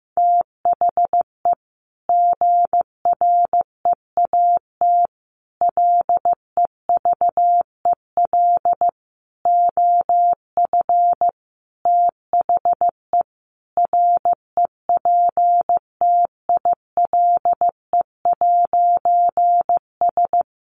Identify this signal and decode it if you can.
Morse code, heard as a keyed tone, about 15 wpm, 700 Hz. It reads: THE GREAT LEVEL OF THE REPTILE'S
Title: A Sound Of Thunder 15wpm